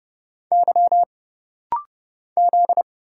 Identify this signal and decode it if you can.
Morse code, heard as a keyed tone, about 30 wpm, 700 Hz. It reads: Y Z
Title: Single Letters - Rapid-Fire 30wpm